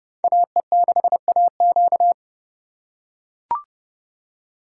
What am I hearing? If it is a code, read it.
Morse code, heard as a keyed tone, about 28 wpm, 700 Hz. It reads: AE6AQ